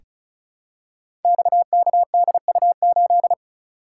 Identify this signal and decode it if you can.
Morse code, heard as a keyed tone, about 35 wpm, 700 Hz. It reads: XKDU8
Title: Code Groups - 5 Letter:Numbers - 35wpm